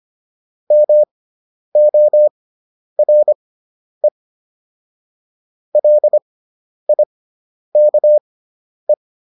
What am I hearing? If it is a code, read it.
Morse code, heard as a keyed tone, about 25 wpm, spaced as for 10 wpm, 600 Hz. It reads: MORE LIKE